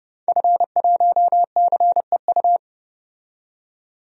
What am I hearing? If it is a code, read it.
Morse code, heard as a keyed tone, about 31 wpm, 700 Hz. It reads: F1CEU
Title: DXCC-100 Call Signs - 5 Characters - 30wpm